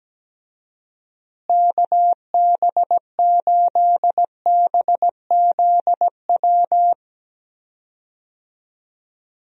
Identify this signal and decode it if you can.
Morse code, heard as a keyed tone, about 17 wpm, 700 Hz. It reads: KB8BZW
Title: USA Call Signs - 6 Characters - Repeated 3x - 17wpm